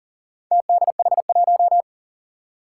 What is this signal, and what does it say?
Morse code, heard as a keyed tone, about 40 wpm, 700 Hz. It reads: TDH1
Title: Four Challenging Letter-Number Combinations 40wpm